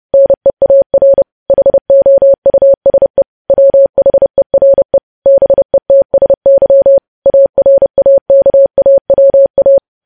Morse code, heard as a keyed tone, about 30 wpm, 550 Hz. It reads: NEAR HOUSE WHERE BETSY ARAKAWA
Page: news headlines in morse code at 30 wpm